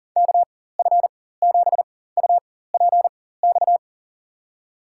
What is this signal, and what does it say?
Morse code, heard as a keyed tone, about 40 wpm, spaced as for 19 wpm, 700 Hz. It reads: KF7UPX
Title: USA Call Signs - 6 Characters - 4x Character Spacing - 40wpm